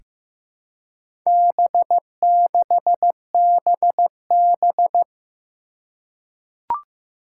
Morse code, heard as a keyed tone, about 15 wpm, 700 Hz. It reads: B6BB
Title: Four Challenging Letter-Number Combinations 15wpm